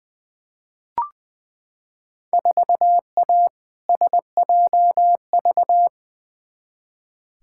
Morse code, heard as a keyed tone, about 20 wpm, 700 Hz. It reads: 4A SJV
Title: Field Day Exchanges Encore - 20wpm